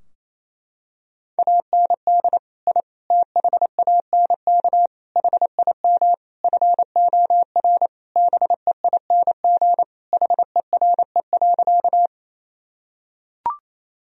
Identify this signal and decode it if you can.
Morse code, heard as a keyed tone, about 28 wpm, 700 Hz. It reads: AND I THANK HIM FOR BEING HERE.
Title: Sentences from Top 1000 Words - Review Part 3 - 28wpm